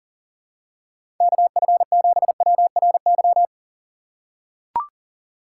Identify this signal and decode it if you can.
Morse code, heard as a keyed tone, about 40 wpm, 700 Hz. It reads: KF7WRY
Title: USA Call Signs - 6 Characters - 3x Character Spacing - 40wpm